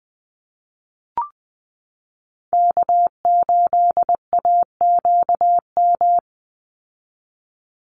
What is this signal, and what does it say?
Morse code, heard as a keyed tone, about 20 wpm, 700 Hz. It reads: K8AQM
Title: Call Signs Repeated 3x - 20wpm